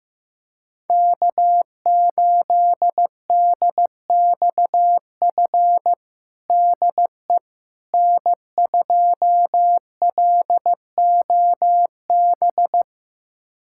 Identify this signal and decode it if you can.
Morse code, heard as a keyed tone, about 15 wpm, 700 Hz. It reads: K8DXF DE N2LOB
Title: USA Callsign DE Callsign - 5 Characters - 15wpm